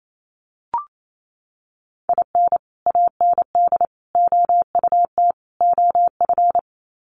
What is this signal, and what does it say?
Morse code, heard as a keyed tone, about 28 wpm, 700 Hz. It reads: IN AND OUT OF
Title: Sets of 4 Words 28wpm